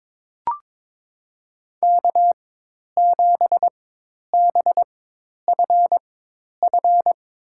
Morse code, heard as a keyed tone, about 22 wpm, spaced as for 10 wpm, 700 Hz. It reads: K7BFF
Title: USA Call Signs - 4x Character Spacing - 22wpm